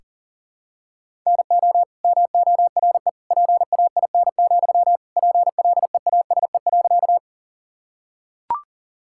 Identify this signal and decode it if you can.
Morse code, heard as a keyed tone, about 40 wpm, 700 Hz. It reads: NO MORE PAIN, PLEASE.
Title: Sentences from Top 1600 Words 40wpm